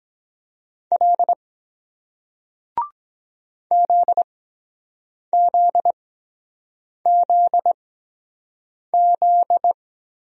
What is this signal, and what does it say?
Morse code, heard as a keyed tone, about 23 wpm, spaced as for 7 wpm, 700 Hz. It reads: L ZZZZ